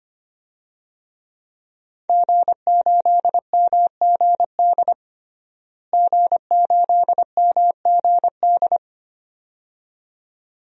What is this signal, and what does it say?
Morse code, heard as a keyed tone, about 25 wpm, 700 Hz. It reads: G8MGD G8MGD
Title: DXCC-100 Call Signs - 5 Characters - Repeated 2x - 25wpm